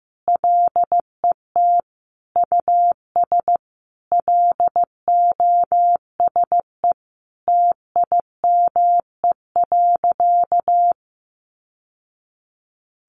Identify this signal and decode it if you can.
Morse code, heard as a keyed tone, about 15 wpm, 700 Hz. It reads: LET US LOSE TIME.
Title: Sentences from Top 1100 Words 15wpm